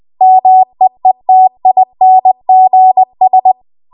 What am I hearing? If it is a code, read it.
Morse code, heard as a keyed tone, about 20 wpm, 750 Hz. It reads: MEETINGS
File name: Morse / 20WPM